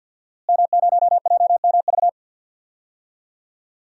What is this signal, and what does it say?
Morse code, heard as a keyed tone, about 52 wpm, 700 Hz. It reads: M0JMV
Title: DXCC-100 Call Signs - 5 Characters - Repeated 3x - 50wpm